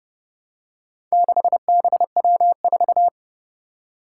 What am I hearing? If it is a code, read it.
Morse code, heard as a keyed tone, about 30 wpm, 700 Hz. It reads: 6BW4
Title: Four Challenging Letter-Number Combinations 30wpm